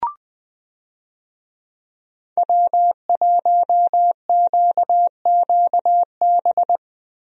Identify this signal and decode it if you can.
Morse code, heard as a keyed tone, about 20 wpm, 700 Hz. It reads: W1QQB
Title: USA Call Signs - 5 Characters - Repeated 2x - 20wpm